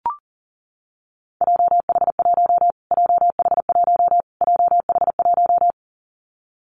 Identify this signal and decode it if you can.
Morse code, heard as a keyed tone, about 40 wpm, 700 Hz. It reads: JH1 JH1 JH1
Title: Callsign Prefixes for Top 25 DXCC - Repeated 3x - 40wpm